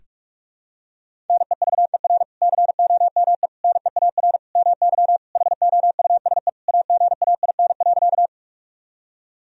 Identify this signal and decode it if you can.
Morse code, heard as a keyed tone, about 46 wpm, 700 Hz. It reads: NEVER COME NEAR MY HOUSE AGAIN.
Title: Sentences from Top 1000 Words - Review Part 1 - 45wpm